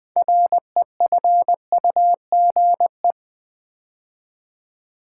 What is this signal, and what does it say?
Morse code, heard as a keyed tone, about 20 wpm, 700 Hz. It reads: REFUGE
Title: 6 Letter Words - First 3 Letters Given - 20wpm